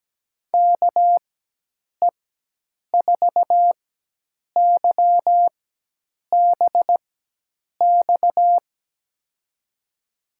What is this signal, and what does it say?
Morse code, heard as a keyed tone, about 17 wpm, spaced as for 8 wpm, 700 Hz. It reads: KE4YBX
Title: USA Call Signs - 6 Characters - 4x Character Spacing - 17wpm